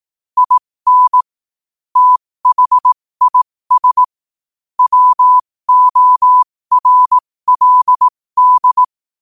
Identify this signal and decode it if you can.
Morse code, heard as a keyed tone, about 18 wpm, spaced as for 15 wpm, 1000 Hz. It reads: IN THIS WORLD
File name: qotdfiles / 15wpm